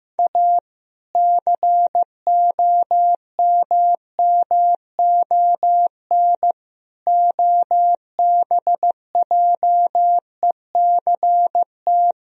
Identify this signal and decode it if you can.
Morse code, heard as a keyed tone, about 15 wpm, 700 Hz. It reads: A COMMON OBJECT
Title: Sentences from Top 1000 Words - Review Part 3 - 15wpm